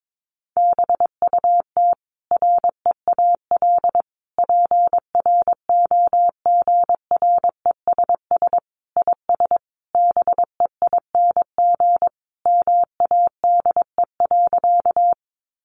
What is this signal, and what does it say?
Morse code, heard as a keyed tone, about 22 wpm, 700 Hz. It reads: BUT REAL PROGRESS IS BEING MADE.
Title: Sentences from Top 1400 Words 22wpm